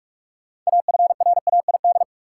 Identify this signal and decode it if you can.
Morse code, heard as a keyed tone, about 46 wpm, 700 Hz. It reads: AFRAID
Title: Sentences from Top 1500 Words - New Word Drill - 45wpm